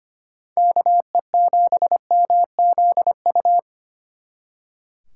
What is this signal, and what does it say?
Morse code, heard as a keyed tone, about 29 wpm, 700 Hz. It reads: KE7MZU
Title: Speed Racing - USA 2x3 Call Signs 25wpm